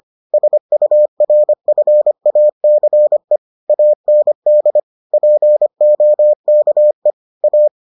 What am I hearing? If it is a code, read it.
Morse code, heard as a keyed tone, about 25 wpm, 600 Hz. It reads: SURFACE AND POKE A